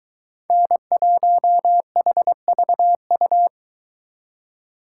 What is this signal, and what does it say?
Morse code, heard as a keyed tone, about 23 wpm, 700 Hz. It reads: N1HVU